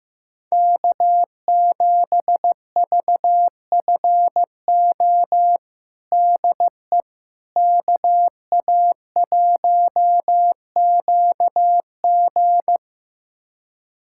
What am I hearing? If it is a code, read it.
Morse code, heard as a keyed tone, about 15 wpm, 700 Hz. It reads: K7VFO DE KA1QG
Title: USA Callsign DE Callsign - 5 Characters - 15wpm